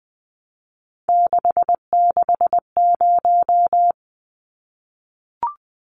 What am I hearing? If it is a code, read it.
Morse code, heard as a keyed tone, about 20 wpm, 700 Hz. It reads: 660
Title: Numbers - 3-Digit 20wpm